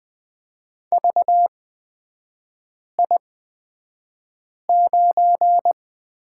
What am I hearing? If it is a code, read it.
Morse code, heard as a keyed tone, about 20 wpm, spaced as for 5 wpm, 690 Hz. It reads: VI9
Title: 255 Lesson 41e - Period - Three Characters 20wpm